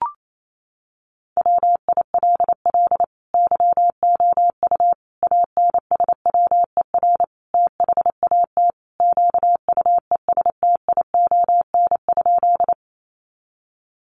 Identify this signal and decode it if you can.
Morse code, heard as a keyed tone, about 28 wpm, 700 Hz. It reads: WILL YOU ANSWER THAT QUESTION?
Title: Sentences from Top 400 Words 28wpm